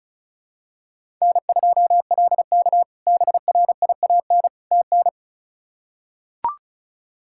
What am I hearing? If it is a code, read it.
Morse code, heard as a keyed tone, about 34 wpm, 700 Hz. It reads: N2LK BRIAN TN
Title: K1USN SST QSO Practice - 35wpm